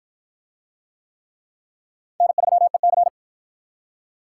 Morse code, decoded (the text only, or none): N3EC